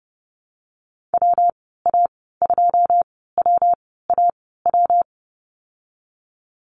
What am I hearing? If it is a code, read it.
Morse code, heard as a keyed tone, about 30 wpm, spaced as for 17 wpm, 700 Hz. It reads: WA2WAW
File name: Lesson 19g - Letter w - Call Signs 30wpm